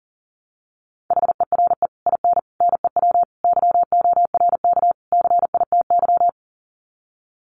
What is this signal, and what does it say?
Morse code, heard as a keyed tone, about 40 wpm, 700 Hz. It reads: HERE IN NEW YORK CITY
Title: Sets of 5 Words Encore2 40wpm